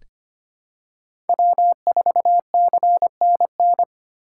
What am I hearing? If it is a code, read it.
Morse code, heard as a keyed tone, about 25 wpm, 700 Hz. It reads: W4CNN
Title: USA Call Signs - 5 Characters - 2x Character Spacing - 25wpm